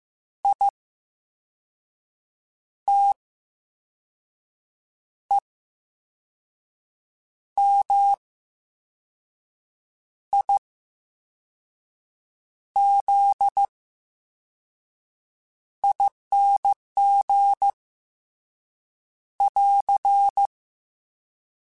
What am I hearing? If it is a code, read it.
Morse code, heard as a keyed tone, about 15 wpm, spaced as for 5 wpm, 800 Hz. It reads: ITEMIZING <AR>
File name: Special INGs-mp3s / Special INGs-mp3s / VerySlowToFaster Ing @13